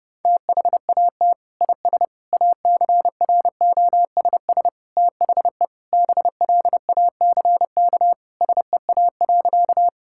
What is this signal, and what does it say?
Morse code, heard as a keyed tone, about 30 wpm, 700 Hz. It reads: THAT IS ACROSS THE BLACK SEA.
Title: Sentences from Top 1000 Words - Review Part 5 - 30wpm